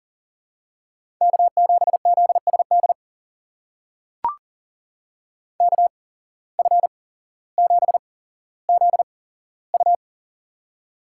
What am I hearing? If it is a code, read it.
Morse code, heard as a keyed tone, about 40 wpm, spaced as for 17 wpm, 700 Hz. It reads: K7ZSD KF7ZU
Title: USA Call Signs - 5 Characters - 8x Character Spacing - 40wpm